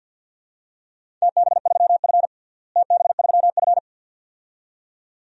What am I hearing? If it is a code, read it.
Morse code, heard as a keyed tone, about 53 wpm, 700 Hz. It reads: TB3F TB3F
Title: DXCC-100 Call Signs - 4 Characters - Repeated 2x - 50wpm